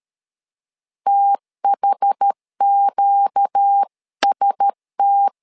Morse code, heard as a keyed tone, about 13 wpm, 800 Hz. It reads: THQST
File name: LEZIONI / 5 LBVC / 5 LB 12wpm 1906